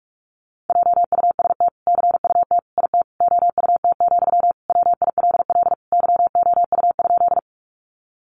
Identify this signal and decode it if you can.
Morse code, heard as a keyed tone, about 44 wpm, 700 Hz. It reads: JUST CUT IT OUT, WILL YOU?
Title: Sentences from Top 300 Words 45wpm